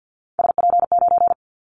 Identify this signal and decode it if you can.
Morse code, heard as a keyed tone, about 49 wpm, 700 Hz. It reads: SP9
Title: Callsign Prefixes for Top 25 DXCC - Repeated 3x - 50wpm